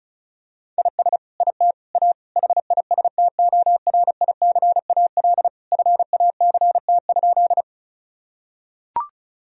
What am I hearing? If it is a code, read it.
Morse code, heard as a keyed tone, about 34 wpm, 700 Hz. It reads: IS IT A HISTORICAL FACT?
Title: Sentences from Top 1000 Words - Review Part 4 - 35wpm